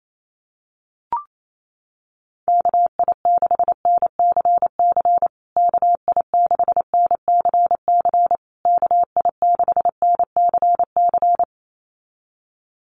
Morse code, heard as a keyed tone, about 28 wpm, 700 Hz. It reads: KI6NCC KI6NCC KI6NCC
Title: USA Call Signs Repeated 3x 28wpm